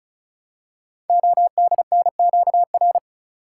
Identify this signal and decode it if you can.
Morse code, heard as a keyed tone, about 36 wpm, 690 Hz. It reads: ODNQR
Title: Code Groups - 5 Letters - 35wpm